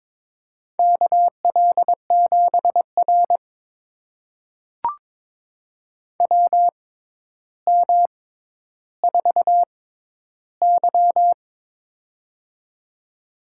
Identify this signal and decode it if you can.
Morse code, heard as a keyed tone, about 23 wpm, spaced as for 11 wpm, 700 Hz. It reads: KL7R WM4Y
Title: USA Call Signs - 4 Characters - 6x Character Spacing - 22wpm